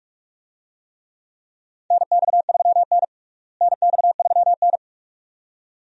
Morse code, heard as a keyed tone, about 46 wpm, 700 Hz. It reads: NX3N NX3N
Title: USA Call Signs - 4 Characters - Repeated 2x - 45wpm